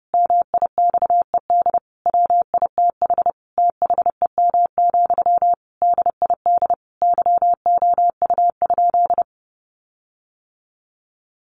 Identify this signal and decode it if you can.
Morse code, heard as a keyed tone, about 30 wpm, 700 Hz. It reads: MIXED WITH THEM, DID YOU?
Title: Sentences from Top 1500 Words 30wpm